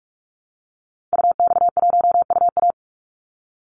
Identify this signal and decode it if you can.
Morse code, heard as a keyed tone, about 44 wpm, 700 Hz. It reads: UX1UA
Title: Call Signs Repeated 3x - 45wpm